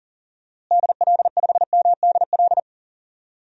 Morse code, heard as a keyed tone, about 38 wpm, 700 Hz. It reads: DL5MDL
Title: DXCC-100 Call Signs - 6 Characters - Repeated 2x - 40wpm